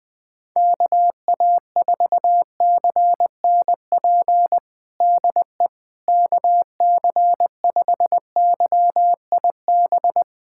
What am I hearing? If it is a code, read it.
Morse code, heard as a keyed tone, about 20 wpm, 700 Hz. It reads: KA4CNP DE KC5YIB
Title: USA Callsign DE Callsign - 6 Characters - 20wpm